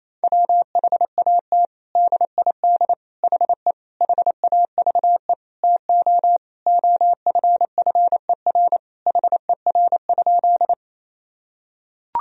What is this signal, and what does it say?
Morse code, heard as a keyed tone, about 28 wpm, 700 Hz. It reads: WHAT DID HE HAVE TO OFFER HER?
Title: Sentences from Top 1000 Words - Review Part 4 - 28wpm